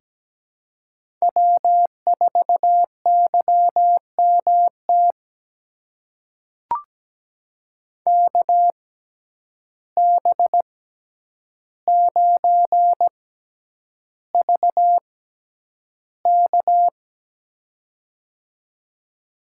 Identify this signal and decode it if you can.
Morse code, heard as a keyed tone, about 17 wpm, spaced as for 9 wpm, 700 Hz. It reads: W4YMT KB9VK
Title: USA Call Signs - 5 Characters - 6x Character Spacing - 17wpm